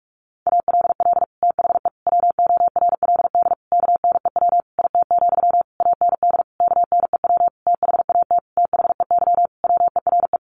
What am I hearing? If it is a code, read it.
Morse code, heard as a keyed tone, about 44 wpm, 700 Hz. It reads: ALL THE WORLD KNEW IT, AND KNEW THAT THEY WERE
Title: Sentences from Top 1200 Words 45wpm